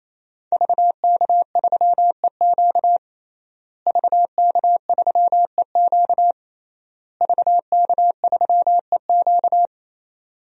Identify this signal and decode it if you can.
Morse code, heard as a keyed tone, about 27 wpm, 700 Hz. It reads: VK3EQ VK3EQ VK3EQ
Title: DXCC-100 Call Signs - 5 Characters - Repeated 3x - 28wpm